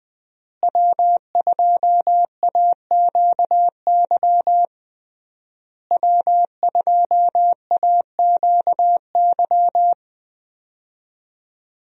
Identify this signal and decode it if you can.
Morse code, heard as a keyed tone, about 20 wpm, 700 Hz. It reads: W2AQY W2AQY
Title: USA Call Signs - 5 Characters - Repeated 3x - 20wpm